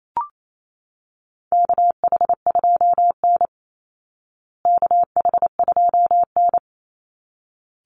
Morse code, heard as a keyed tone, about 28 wpm, 700 Hz. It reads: KH2N KH2N
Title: Call Signs Repeated 3x - 28wpm